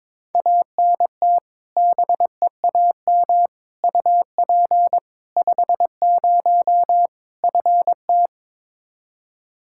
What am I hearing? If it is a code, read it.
Morse code, heard as a keyed tone, about 22 wpm, 700 Hz. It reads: ANT BEAM UP 50 FT
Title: Practice QSO - Exchange 2 - Info - 22wpm